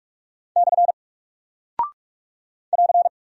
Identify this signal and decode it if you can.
Morse code, heard as a keyed tone, about 44 wpm, 700 Hz. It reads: / <AR>
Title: Single Character (Letter:Number:Punctuation:Prosign) 45wpm